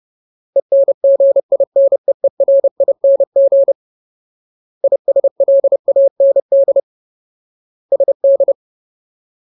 Morse code, heard as a keyed tone, about 30 wpm, 550 Hz. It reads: ENGINEERING ISLAND SD
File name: Ultimate Word List / 30WPM Ultimate